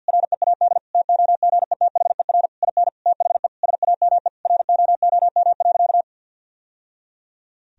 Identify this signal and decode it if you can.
Morse code, heard as a keyed tone, about 50 wpm, 700 Hz. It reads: READ TOGETHER IN THE SAME ROOM.
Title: Sentences from Top 1000 Words - Review Part 3 - 50wpm